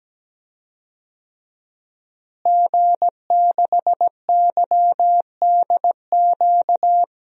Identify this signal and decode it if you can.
Morse code, heard as a keyed tone, about 17 wpm, 700 Hz. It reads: G6YDQ
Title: DXCC-100 Call Signs - 5 Characters - Repeated 2x - 17wpm